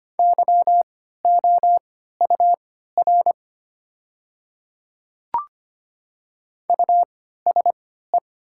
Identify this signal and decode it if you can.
Morse code, heard as a keyed tone, about 25 wpm, spaced as for 14 wpm, 700 Hz. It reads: YOUR USE
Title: 3x Character Spacing - Top 500 Words - 25wpm